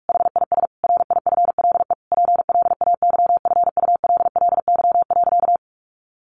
Morse code, heard as a keyed tone, about 44 wpm, 700 Hz. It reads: HIS RIFLE PLAYFULLY.
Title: A Sound Of Thunder 45wpm